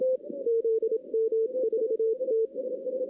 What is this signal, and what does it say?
Morse code, heard as a keyed tone, about 27 wpm, 450 Hz. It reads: ZM4T
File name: 2016 arrl dx cw / 2016-ARRL-DX-CW-K5TR- - 0-25